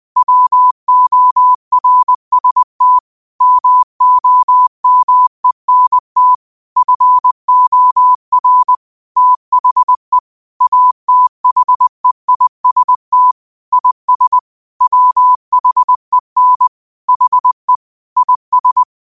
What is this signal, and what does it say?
Morse code, heard as a keyed tone, about 20 wpm, 1000 Hz. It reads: WORST MOMENT FOR THE ATHEIST IS WHEN HE IS